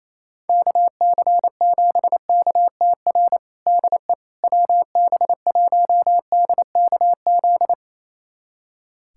Morse code, heard as a keyed tone, about 28 wpm, 700 Hz. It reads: KC7KTR DE WB1DKZ
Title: USA Callsign DE Callsign 28wpm